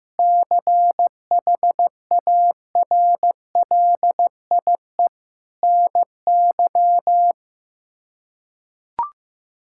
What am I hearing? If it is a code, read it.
Morse code, heard as a keyed tone, about 15 wpm, 700 Hz. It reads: CHARLIE NY
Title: K1USN SST QSO Practice - 15wpm